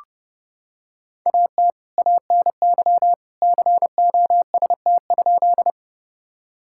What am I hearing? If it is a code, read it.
Morse code, heard as a keyed tone, about 32 wpm, 700 Hz. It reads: AT ANY COST?
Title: Sentences from Top 1000 Words - Review Part 5 - 30wpm